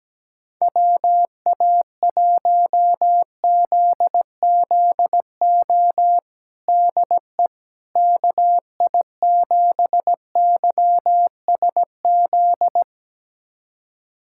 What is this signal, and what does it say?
Morse code, heard as a keyed tone, about 17 wpm, 700 Hz. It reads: WA1ZZO DE KI7YSZ
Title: USA Callsign DE Callsign - 6 Characters - 17wpm